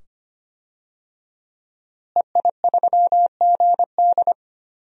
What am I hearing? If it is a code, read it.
Morse code, heard as a keyed tone, about 25 wpm, 700 Hz. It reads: EI3GD
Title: DXCC-100 Call Signs - 5 Characters - Repeated 2x - 25wpm